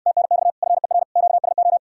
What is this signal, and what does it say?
Morse code, heard as a keyed tone, about 69 wpm, 700 Hz. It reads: TNX FER QSO